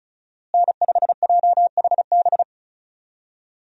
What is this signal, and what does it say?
Morse code, heard as a keyed tone, about 36 wpm, 700 Hz. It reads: N5JHB
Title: USA Call Signs - 5 Characters - Repeated 3x - 35wpm